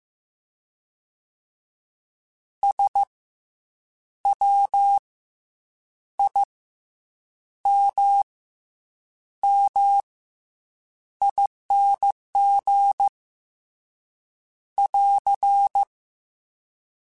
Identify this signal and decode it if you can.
Morse code, heard as a keyed tone, about 15 wpm, 800 Hz. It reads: S W I M M ING <AR>
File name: Special INGs / Special INGs-mp3s / SlowToFaster ing @13